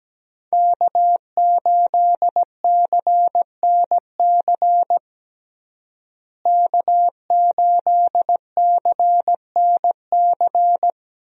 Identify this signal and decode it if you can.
Morse code, heard as a keyed tone, about 17 wpm, 700 Hz. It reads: K8CNC K8CNC
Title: USA Call Signs - 5 Characters - Repeated 3x - 17wpm